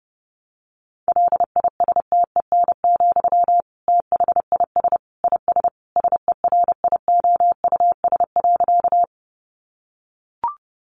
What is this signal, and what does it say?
Morse code, heard as a keyed tone, about 30 wpm, 700 Hz. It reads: LISTEN, THIS IS SERIOUS.
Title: Sentences from Top 1100 Words 30wpm